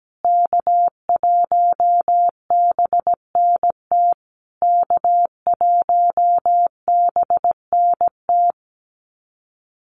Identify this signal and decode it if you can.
Morse code, heard as a keyed tone, about 17 wpm, 700 Hz. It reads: K1BNT K1BNT
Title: USA Call Signs Repeated 3x 17wpm